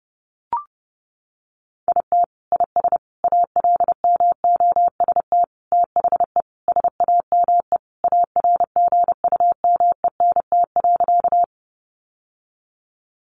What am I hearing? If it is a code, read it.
Morse code, heard as a keyed tone, about 30 wpm, 700 Hz. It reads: IT IS ALMOST THE SAME ARGUMENT.
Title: Sentences from Top 1400 Words 30wpm